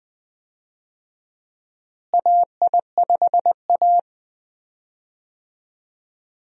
Morse code, heard as a keyed tone, about 20 wpm, 700 Hz. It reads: AI5A
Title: USA Call Signs - 4 Characters - Repeated 2x - 20wpm